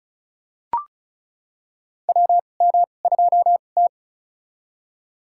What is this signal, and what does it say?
Morse code, heard as a keyed tone, about 36 wpm, spaced as for 25 wpm, 700 Hz. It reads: WM2T
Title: USA Call Signs - 4 Characters - 2x Character Spacing - 35wpm